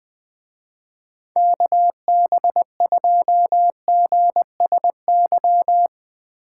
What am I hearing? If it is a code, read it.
Morse code, heard as a keyed tone, about 20 wpm, 700 Hz. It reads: KB2GSY